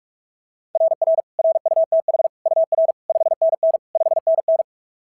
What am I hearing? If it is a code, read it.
Morse code, heard as a keyed tone, about 43 wpm, 690 Hz. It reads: RR RUTH UR 5NN 5NN
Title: SOTA QSO Practice 45wpm